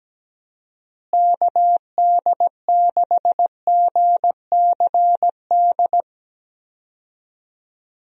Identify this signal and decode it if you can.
Morse code, heard as a keyed tone, about 17 wpm, 700 Hz. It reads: KD6GCD
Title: USA Call Signs - 6 Characters - Repeated 2x - 17wpm